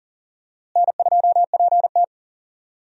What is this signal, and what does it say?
Morse code, heard as a keyed tone, about 39 wpm, 700 Hz. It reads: N2PT
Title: USA Call Signs - 4 Characters - Repeated 2x - 40wpm